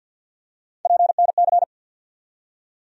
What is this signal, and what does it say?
Morse code, heard as a keyed tone, about 47 wpm, 700 Hz. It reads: PNC
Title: USA Call Sign Postfixes - 3 Letters 50wpm